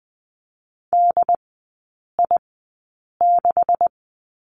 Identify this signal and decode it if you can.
Morse code, heard as a keyed tone, about 20 wpm, spaced as for 8 wpm, 700 Hz. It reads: DI6
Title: 184 Lesson 30e - Letter b - Three Characters 20wpm